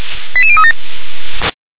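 Five-tone selective call (ZVEI), heard as quick runs of five tones, tones of about 70 ms. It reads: 88037
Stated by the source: Received by SDR, ZVEI five-tone call 88037.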